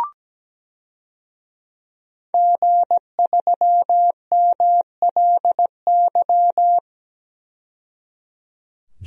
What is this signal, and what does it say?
Morse code, heard as a keyed tone, about 17 wpm, 700 Hz. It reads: G3MLY
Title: DXCC-100 Call Signs - 5 Characters - 17wpm